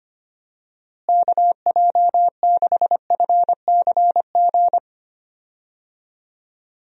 Morse code, heard as a keyed tone, about 25 wpm, 700 Hz. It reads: KJ6FCG
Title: USA Call Signs - 6 Characters - Repeated 3x - 25wpm